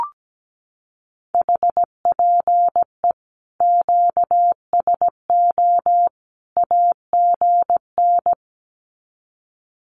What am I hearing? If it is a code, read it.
Morse code, heard as a keyed tone, about 17 wpm, 700 Hz. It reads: HPE QSO AGN
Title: Common QSO Phrases 17wpm